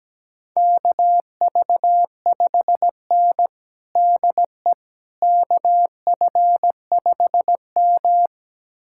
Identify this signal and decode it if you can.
Morse code, heard as a keyed tone, about 17 wpm, 700 Hz. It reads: KV5N DE KF5M
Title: USA Callsign DE Callsign - 4 Characters - 17wpm